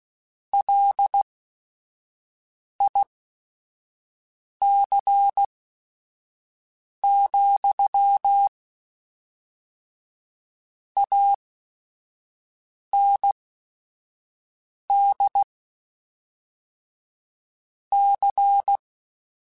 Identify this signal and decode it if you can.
Morse code, heard as a keyed tone, about 16 wpm, spaced as for 5 wpm, 800 Hz. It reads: LIC, AND C
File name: Technical Subjects / Technical Subjects-mp3s / XL @5